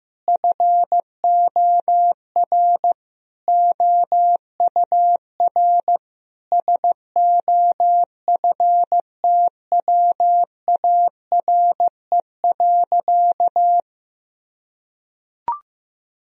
Morse code, heard as a keyed tone, about 15 wpm, 700 Hz. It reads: FOR OUR SOFTWARE.
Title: Sentences from Top 1000 Words - Review Part 3 - 15wpm